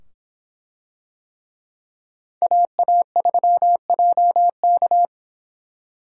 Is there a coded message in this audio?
Yes: Morse code from 2.4 to 5.0 s, heard as a keyed tone, about 25 wpm, 700 Hz. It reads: AA3JK